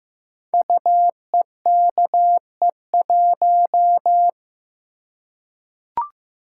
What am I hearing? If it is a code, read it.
Morse code, heard as a keyed tone, about 15 wpm, 700 Hz. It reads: UEKE1
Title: Code Groups - 5 Letter:Numbers - 15wpm